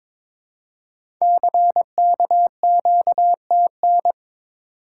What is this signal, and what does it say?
Morse code, heard as a keyed tone, about 23 wpm, 700 Hz. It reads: CKQTN
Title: Code Groups - 5 Letters - 22wpm